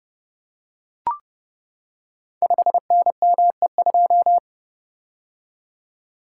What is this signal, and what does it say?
Morse code, heard as a keyed tone, about 30 wpm, 700 Hz. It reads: 5NME2
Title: Code Groups - 5 Letter:Numbers - 30wpm